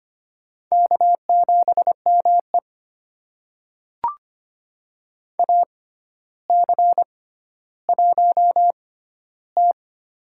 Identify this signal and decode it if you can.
Morse code, heard as a keyed tone, about 25 wpm, spaced as for 13 wpm, 700 Hz. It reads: K7ME AC1T
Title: USA Call Signs - 4 Characters - 6x Character Spacing - 25wpm